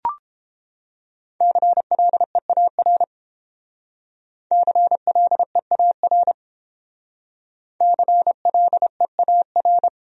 Morse code, heard as a keyed tone, about 28 wpm, 700 Hz. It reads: CLEAR CLEAR CLEAR